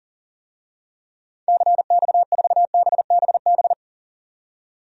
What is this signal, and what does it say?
Morse code, heard as a keyed tone, about 40 wpm, 700 Hz. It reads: CX4BBB
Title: DXCC-100 Call Signs - 6 Characters - 40wpm